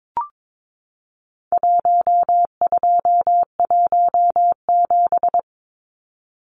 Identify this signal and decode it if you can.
Morse code, heard as a keyed tone, about 22 wpm, 700 Hz. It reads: 1217
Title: Numbers - Contest Style - 4-Digit 22wpm